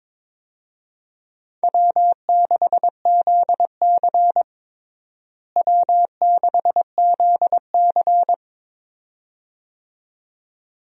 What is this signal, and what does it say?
Morse code, heard as a keyed tone, about 22 wpm, 700 Hz. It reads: W6ZC W6ZC
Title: USA Call Signs - 4 Characters - Repeated 2x - 22wpm